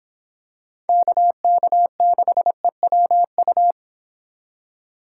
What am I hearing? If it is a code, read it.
Morse code, heard as a keyed tone, about 26 wpm, 700 Hz. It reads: KK6EWU